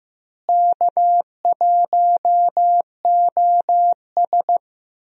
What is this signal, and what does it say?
Morse code, heard as a keyed tone, about 15 wpm, 700 Hz. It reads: K1OS
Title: USA Call Signs - 4 Characters - Repeated 3x - 15wpm